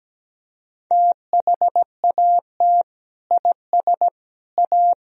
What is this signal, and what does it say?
Morse code, heard as a keyed tone, about 17 wpm, 700 Hz. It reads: THAT IS A
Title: Sentences from Top 1000 Words - Review Part 4 - 17wpm